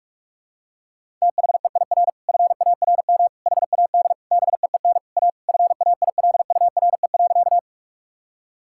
Morse code, heard as a keyed tone, about 41 wpm, 700 Hz. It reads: THEIR FARM HAD BEEN A FAILURE.
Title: Sentences from Top 1000 Words - Review Part 6 - 45wpm